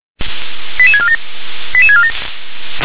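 Five-tone selective call (ZVEI), heard as quick runs of five tones, tones of about 70 ms. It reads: 88647 88647
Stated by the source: ZVEI five-tone selective call 88085 received on an SDR.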